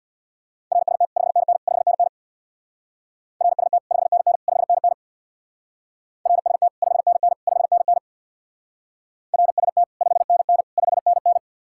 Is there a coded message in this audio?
Yes: Morse code from 0.7 to 11.4 s, heard as a keyed tone, about 65 wpm, 700 Hz. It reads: RST 5NN 5NN RST 5NN 5NN RST 5NN 5NN RST 5NN 5NN